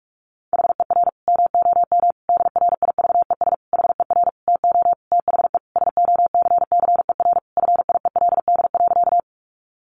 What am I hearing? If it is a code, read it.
Morse code, heard as a keyed tone, about 44 wpm, 700 Hz. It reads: HER MOM DRIVES HER TO THE SOCCER FIELD.
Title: Easy Phrases 45wpm